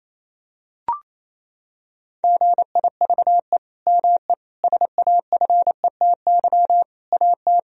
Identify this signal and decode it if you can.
Morse code, heard as a keyed tone, about 28 wpm, 700 Hz. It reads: GIVE ME SAFETY AT
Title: Sentences from Top 1000 Words - Review Part 4 - 28wpm